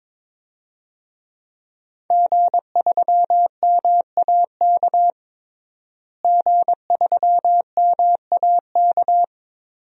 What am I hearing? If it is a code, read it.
Morse code, heard as a keyed tone, about 21 wpm, 700 Hz. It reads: G3MAK G3MAK
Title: DXCC-100 Call Signs - 5 Characters - Repeated 2x - 22wpm